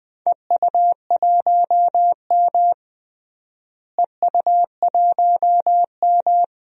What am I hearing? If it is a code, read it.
Morse code, heard as a keyed tone, about 20 wpm, 700 Hz. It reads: EU1M EU1M
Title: DXCC-100 Call Signs - 4 Characters - Repeated 2x - 20wpm